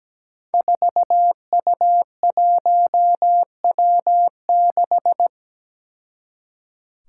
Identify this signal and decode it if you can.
Morse code, heard as a keyed tone, about 17 wpm, 700 Hz. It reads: 4U1W6
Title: Five Challenging Letter-Number Combinations 17wpm